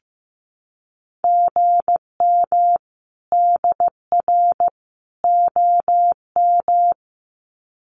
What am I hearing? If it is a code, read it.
Morse code, heard as a keyed tone, about 15 wpm, 700 Hz. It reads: GM DR OM
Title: Common QSO Phrases 15wpm